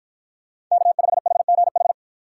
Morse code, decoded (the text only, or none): K5HGH